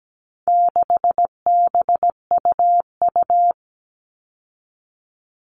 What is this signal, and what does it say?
Morse code, heard as a keyed tone, about 17 wpm, 700 Hz. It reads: 6BUU
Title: All Cut Numbers - 4-Digit 17wpm